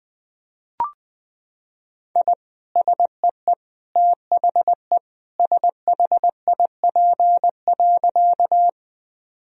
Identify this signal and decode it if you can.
Morse code, heard as a keyed tone, about 20 wpm, 700 Hz. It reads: I SEE THE SHIP.
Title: Sentences from Top 1000 Words - Review Part 2 - 20wpm